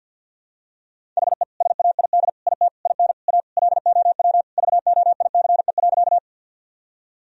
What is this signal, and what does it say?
Morse code, heard as a keyed tone, about 52 wpm, 700 Hz. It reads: HE SAID IT IN A LOW VOICE.
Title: Sentences from Top 1000 Words - Review Part 1 - 50wpm